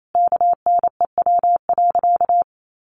Morse code, heard as a keyed tone, about 28 wpm, 700 Hz. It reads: KNEW.
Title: Sentences from Top 400 Words 28wpm